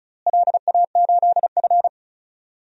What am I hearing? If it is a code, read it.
Morse code, heard as a keyed tone, about 34 wpm, 700 Hz. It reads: LA8F
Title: DXCC-100 Call Signs - 4 Characters - Repeated 2x - 35wpm